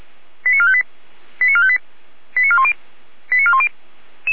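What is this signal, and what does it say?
Five-tone selective call (ZVEI), heard as quick runs of five tones, tones of about 70 ms. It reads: 89458 89458 89410 89410
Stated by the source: ZVEI five-tone call 89001 received by SDR, from the feuerwehr-orlamuende.org fire brigade archive.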